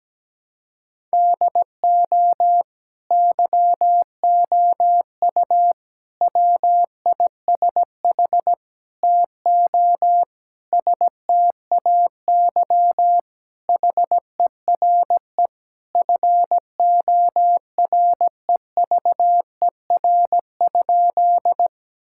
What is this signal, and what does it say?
Morse code, heard as a keyed tone, about 17 wpm, 700 Hz. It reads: DO YOU WISH TO STAY HERE FOREVER?
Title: Sentences from Top 1600 Words 17wpm